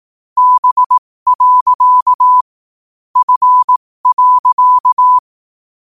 Morse code, heard as a keyed tone, about 18 wpm, spaced as for 15 wpm, 1000 Hz. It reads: B. F.
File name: qotdfiles / 15wpm